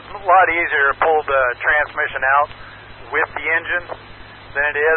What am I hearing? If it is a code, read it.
Emergency Alert System (SAME) tones, heard as harsh buzzing bursts, with speech. It is heard from 0.5 to 3.9 s.